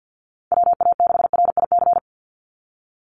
Morse code, heard as a keyed tone, about 48 wpm, 700 Hz. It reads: WA6RIC
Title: NA SOTA Chaser Callsigns 50wpm